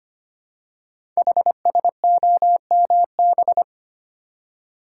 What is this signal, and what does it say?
Morse code, heard as a keyed tone, about 25 wpm, 700 Hz. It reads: HSOMB
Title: Code Groups - 5 Letters - 25wpm